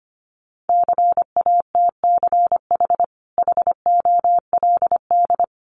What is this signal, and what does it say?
Morse code, heard as a keyed tone, about 25 wpm, 700 Hz. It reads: CATCH HOLD
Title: Sentences from Top 1500 Words 25wpm